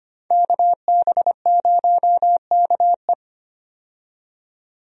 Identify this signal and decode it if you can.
Morse code, heard as a keyed tone, about 25 wpm, 700 Hz. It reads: KB0KE
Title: USA Call Signs - 5 Characters - 25wpm